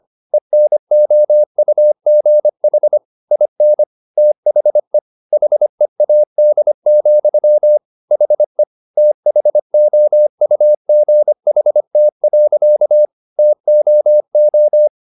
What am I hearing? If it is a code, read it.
Morse code, heard as a keyed tone, about 25 wpm, 600 Hz. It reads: ENOUGH IN THE HEAD, HE THOUGHT. TOO